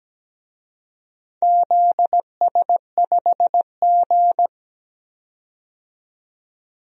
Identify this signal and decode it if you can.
Morse code, heard as a keyed tone, about 17 wpm, 700 Hz. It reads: ZS5G
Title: DXCC-100 Call Signs - 4 Characters - 17wpm